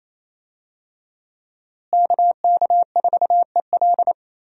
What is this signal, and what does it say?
Morse code, heard as a keyed tone, about 28 wpm, 700 Hz. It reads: KK4EL
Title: USA Call Signs - 5 Characters - Repeated 3x - 28wpm